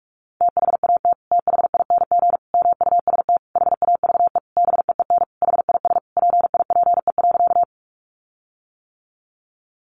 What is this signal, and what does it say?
Morse code, heard as a keyed tone, about 44 wpm, 700 Hz. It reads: THAT THING MUST HAVE BEEN HIS PIPE.
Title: Sentences from Top 1500 Words 45wpm